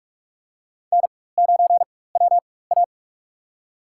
Morse code, heard as a keyed tone, about 44 wpm, spaced as for 21 wpm, 700 Hz. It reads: N9WA
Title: USA Call Signs - 4 Characters - 4x Character Spacing - 45wpm